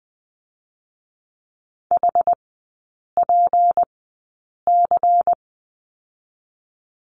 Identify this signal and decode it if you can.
Morse code, heard as a keyed tone, about 19 wpm, spaced as for 8 wpm, 700 Hz. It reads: HPC
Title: 152 Lesson 25e - Letter p - Three Characters 20wpm